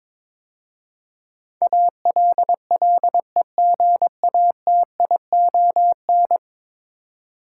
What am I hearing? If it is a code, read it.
Morse code, heard as a keyed tone, about 22 wpm, 700 Hz. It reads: ALLEGATION
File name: practice / 22wpm / 10 Letter Words - First 3 Letters Given - 22wpm